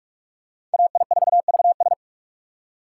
Morse code, heard as a keyed tone, about 46 wpm, 700 Hz. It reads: AI4VS